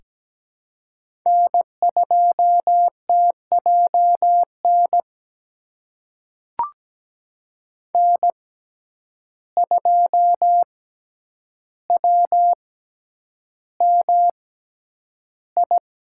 Morse code, heard as a keyed tone, about 17 wpm, spaced as for 9 wpm, 700 Hz. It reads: N2TJN N2WMI
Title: USA Call Signs - 5 Characters - 6x Character Spacing - 17wpm